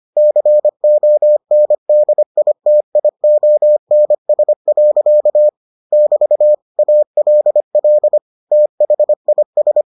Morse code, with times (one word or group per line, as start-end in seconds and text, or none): 0.2-5.5 CONDITIONS.
5.9-6.5 <BT>
6.8-8.2 ALL
8.5-9.8 THIS